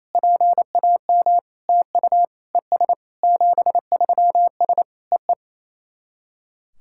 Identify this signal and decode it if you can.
Morse code, heard as a keyed tone, about 27 wpm, 690 Hz. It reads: PAM TU ES 73S EE
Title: SOTA QSO Practice 28wpm